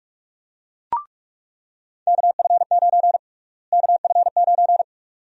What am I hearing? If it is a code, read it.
Morse code, heard as a keyed tone, about 46 wpm, 700 Hz. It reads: KF9 KF9
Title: USA Call Sign Prefixes - Repeated 2x - 45wpm